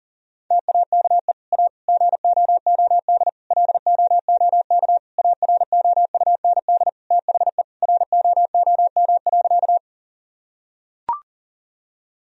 Morse code, heard as a keyed tone, about 40 wpm, 700 Hz. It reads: TAKE A GOOD LOOK AROUND THE ROOM.
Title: Sentences from Top 1000 Words - Review Part 3 - 40wpm